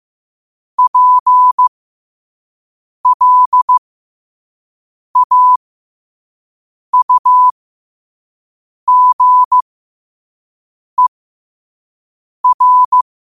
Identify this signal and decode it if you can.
Morse code, heard as a keyed tone, about 15 wpm, spaced as for 5 wpm, 1000 Hz. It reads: PLAUGER
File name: 5wpm